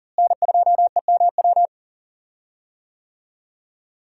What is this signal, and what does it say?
Morse code, heard as a keyed tone, about 40 wpm, 700 Hz. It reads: N2EMW
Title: USA Call Signs - 5 Characters - Repeated 2x - 40wpm